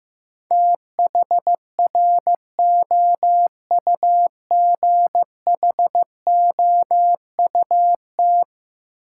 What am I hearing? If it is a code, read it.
Morse code, heard as a keyed tone, about 15 wpm, 700 Hz. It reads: THROUGHOUT
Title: Sentences from Top 800 Words - New Word Drill - 15wpm